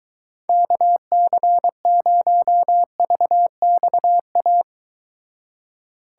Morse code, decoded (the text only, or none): KC0VXA